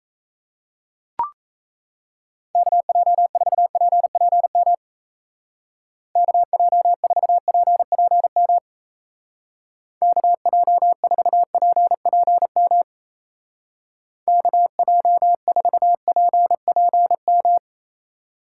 Morse code, heard as a keyed tone, about 34 wpm, 700 Hz. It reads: KJ4PPM KJ4PPM KJ4PPM KJ4PPM
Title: Speed Racing - USA Call Signs 28wpm